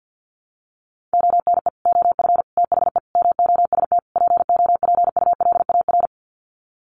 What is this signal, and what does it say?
Morse code, heard as a keyed tone, about 50 wpm, 700 Hz. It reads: ONE OF THE MOST POPULAR
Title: Sets of 5 Words 50wpm